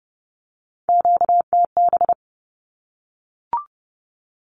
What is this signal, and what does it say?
Morse code, heard as a keyed tone, about 30 wpm, 700 Hz. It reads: QTB
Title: Q-Codes 30wpm